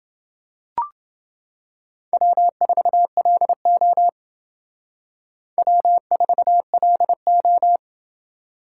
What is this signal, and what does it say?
Morse code, heard as a keyed tone, about 28 wpm, 700 Hz. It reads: W4LO W4LO